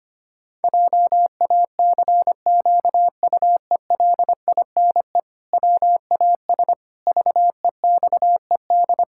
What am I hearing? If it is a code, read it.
Morse code, heard as a keyed tone, about 25 wpm, 700 Hz. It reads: JACQUELINE WAS VEXED
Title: Pangram Challenge v2 - 25wpm